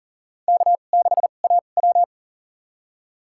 Morse code, heard as a keyed tone, about 40 wpm, spaced as for 29 wpm, 700 Hz. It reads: K6AW
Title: USA Call Signs - 4 Characters - 2x Character Spacing - 40wpm